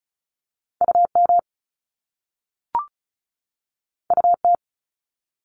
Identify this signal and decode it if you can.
Morse code, heard as a keyed tone, about 36 wpm, 700 Hz. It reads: UM UT
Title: Top Two Letter Combinations in Words 35wpm